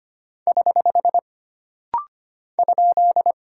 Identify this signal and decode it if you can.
Morse code, heard as a keyed tone, about 25 wpm, 700 Hz. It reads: <HH> ?
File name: Single Punctuation-Prosign - Rapid-Fire - 25wpm